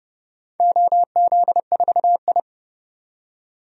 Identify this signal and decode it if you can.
Morse code, heard as a keyed tone, about 31 wpm, 700 Hz. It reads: OZ4I
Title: DXCC-100 Call Signs - 4 Characters - 30wpm